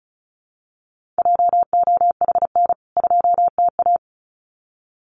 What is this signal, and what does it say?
Morse code, heard as a keyed tone, about 34 wpm, 700 Hz. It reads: JOHN 2TA
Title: CWT Exchange 35wpm